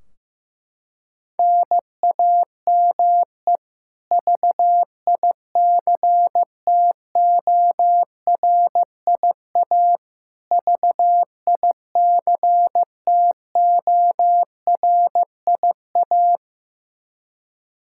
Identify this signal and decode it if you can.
Morse code, heard as a keyed tone, about 16 wpm, 700 Hz. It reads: NAME VICTORIA VICTORIA
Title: K1USN SST QSO Practice 15wpm